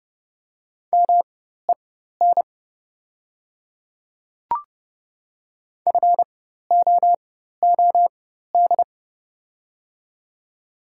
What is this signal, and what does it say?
Morse code, heard as a keyed tone, about 30 wpm, spaced as for 14 wpm, 700 Hz. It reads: MEN FOOD
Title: 4x Character Spacing - Top 300 Words - 30wpm